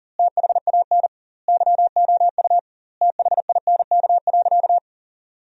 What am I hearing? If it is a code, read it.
Morse code, heard as a keyed tone, about 40 wpm, 700 Hz. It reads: THAN YOU THINK.
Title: 2x Word-Spacing - Sentences from Top 400 Words - 40wpm